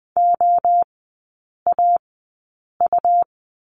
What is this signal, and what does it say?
Morse code, heard as a keyed tone, about 20 wpm, spaced as for 8 wpm, 700 Hz. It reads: OAU
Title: 100 Lesson 17e - Letter u - Three Characters 20wpm